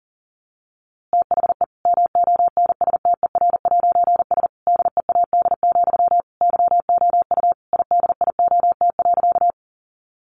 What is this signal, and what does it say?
Morse code, heard as a keyed tone, about 40 wpm, 700 Hz. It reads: THE MONSTER'S DEAD, YOU IDIOT.
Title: A Sound Of Thunder 40wpm